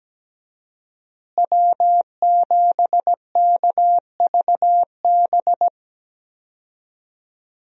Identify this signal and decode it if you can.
Morse code, heard as a keyed tone, about 17 wpm, 700 Hz. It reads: W7KVB
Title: USA Call Signs - 5 Characters - Repeated 3x - 17wpm